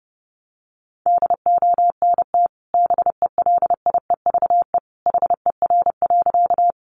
Morse code, heard as a keyed tone, about 30 wpm, 700 Hz. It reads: DONT BELIEVE HER.
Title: Sentences from Top 1000 Words 30wpm